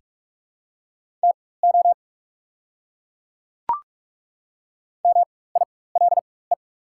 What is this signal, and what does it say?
Morse code, heard as a keyed tone, about 44 wpm, spaced as for 21 wpm, 690 Hz. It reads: TO MILE
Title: 4x Character Spacing - Top 300 Words - 45wpm